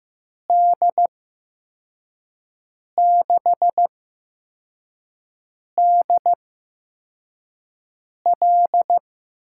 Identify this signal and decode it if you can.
Morse code, heard as a keyed tone, about 15 wpm, spaced as for 4 wpm, 700 Hz. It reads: D6DL
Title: USA Call Signs - 6 Characters - 8x Character Spacing - 15wpm